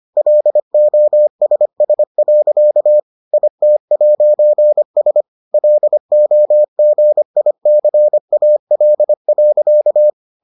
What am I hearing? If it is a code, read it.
Morse code, heard as a keyed tone, about 25 wpm, 600 Hz. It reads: LOSS. IT'S LOGICAL.